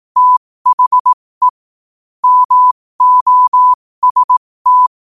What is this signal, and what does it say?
Morse code, heard as a keyed tone, about 18 wpm, spaced as for 15 wpm, 1000 Hz. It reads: THE MOST
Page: 15 wpm morse code quotes